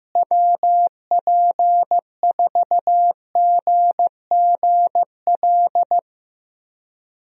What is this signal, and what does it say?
Morse code, heard as a keyed tone, about 16 wpm, 700 Hz. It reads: WP4GGL
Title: USA Call Signs - 6 Characters - Repeated 3x - 15wpm